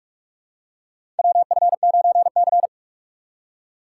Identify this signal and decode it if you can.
Morse code, heard as a keyed tone, about 46 wpm, 700 Hz. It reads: WF9C
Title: USA Call Signs - 4 Characters - 3x Character Spacing - 45wpm